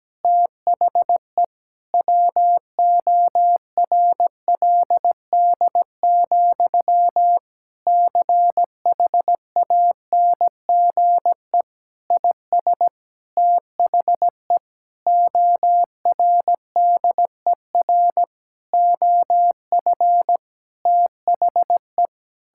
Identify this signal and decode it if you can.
Morse code, heard as a keyed tone, about 17 wpm, 700 Hz. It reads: THE WORLD, CHANGE IS THE ORDER OF THE
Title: Sentences from Top 1000 Words - Review Part 5 - 17wpm